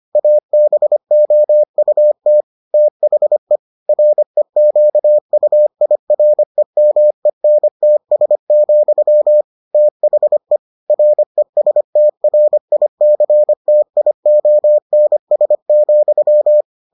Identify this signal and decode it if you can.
Morse code, heard as a keyed tone, about 25 wpm, 600 Hz. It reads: ABOUT THE REQUIREMENTS, THE RESTRICTIONS,